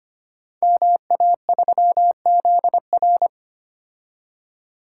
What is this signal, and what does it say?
Morse code, heard as a keyed tone, about 25 wpm, 700 Hz. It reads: MA3ZR
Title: Code Groups - 5 Letter:Numbers - 25wpm